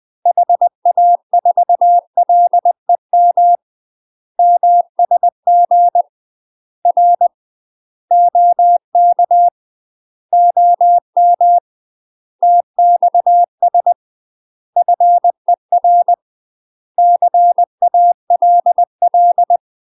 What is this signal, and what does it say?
Morse code, heard as a keyed tone, about 20 wpm, 700 Hz. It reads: HA4LEM MSG R OK OM TXS FER CALL
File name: UK MASTER QSOs TRAINING / 20WPM / UK MASTER QSOs 0069